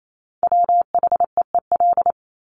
Morse code, heard as a keyed tone, about 28 wpm, 700 Hz. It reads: WHEEL
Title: Top 500 Words Repeated 28wpm